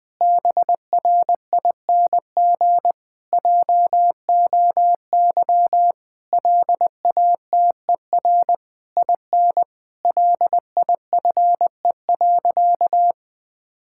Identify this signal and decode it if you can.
Morse code, heard as a keyed tone, about 20 wpm, 700 Hz. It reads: BRING JOY LATER IN LIFE.
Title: Difficult Sentences with Hints - 20wpm